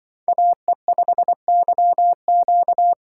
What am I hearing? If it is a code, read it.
Morse code, heard as a keyed tone, about 24 wpm, 700 Hz. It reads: AE5YQ
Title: Speed Racing - USA 2x2 Call Signs 20wpm